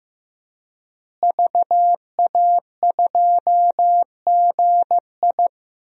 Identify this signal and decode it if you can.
Morse code, heard as a keyed tone, about 15 wpm, 700 Hz. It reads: VA2GI
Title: DXCC-100 Call Signs - 5 Characters - Repeated 2x - 15wpm